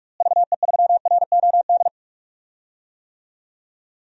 Morse code, heard as a keyed tone, about 42 wpm, 700 Hz. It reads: VE3ROD
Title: DXCC-100 Call Signs - 6 Characters - Repeated 3x - 45wpm